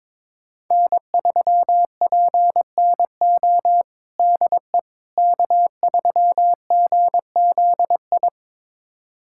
Morse code, heard as a keyed tone, about 21 wpm, 700 Hz. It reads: N3PNO DE K3GZI
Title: USA Callsign DE Callsign - 5 Characters - 22wpm